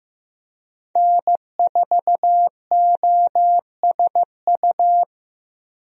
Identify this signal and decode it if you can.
Morse code, heard as a keyed tone, about 15 wpm, 700 Hz. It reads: N4OSU
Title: USA Call Signs - 5 Characters - 15wpm